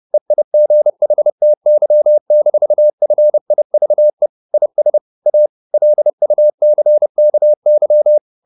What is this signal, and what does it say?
Morse code, heard as a keyed tone, about 30 wpm, 600 Hz. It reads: EIGHTY-FIVE IS A LUCKY
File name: the old man and the sea / the old man and the sea 1 30wpm